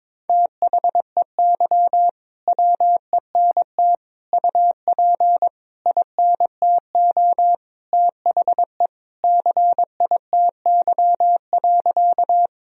Morse code, heard as a keyed tone, about 22 wpm, 700 Hz. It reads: THEY WENT UP INTO THE CITY.
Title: Sentences from Top 1000 Words - Review Part 1 - 22wpm